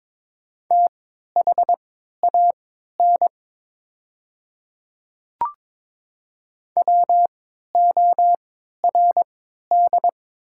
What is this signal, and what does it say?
Morse code, heard as a keyed tone, about 22 wpm, spaced as for 13 wpm, 690 Hz. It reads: THAN WORD
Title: 3x Character Spacing - Top 200 Words - 22wpm